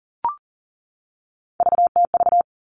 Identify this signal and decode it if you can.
Morse code, heard as a keyed tone, about 40 wpm, 700 Hz. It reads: VTV
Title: All Cut Numbers - 3-Digit 40wpm